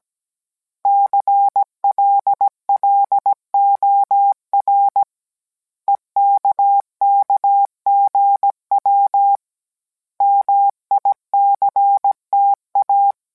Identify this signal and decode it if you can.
Morse code, heard as a keyed tone, about 17 wpm, 800 Hz. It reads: CLLOR EKKGW MICTA